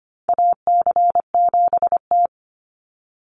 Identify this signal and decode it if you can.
Morse code, heard as a keyed tone, about 25 wpm, 700 Hz. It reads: AC7T
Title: Call Signs - 25wpm